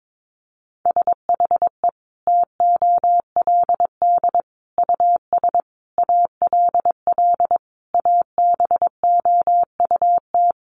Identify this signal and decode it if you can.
Morse code, heard as a keyed tone, about 22 wpm, 700 Hz. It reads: SHE TOLD US ALL ABOUT
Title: Sentences from Top 1200 Words 22wpm